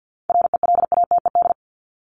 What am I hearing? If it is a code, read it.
Morse code, heard as a keyed tone, about 50 wpm, 700 Hz. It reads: RELATED
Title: Top 751-1000 Words 50wpm